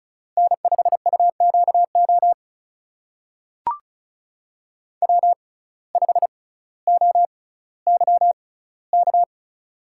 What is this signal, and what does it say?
Morse code, heard as a keyed tone, about 36 wpm, spaced as for 18 wpm, 700 Hz. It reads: N5UQO W5OYK
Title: USA Call Signs - 5 Characters - 6x Character Spacing - 35wpm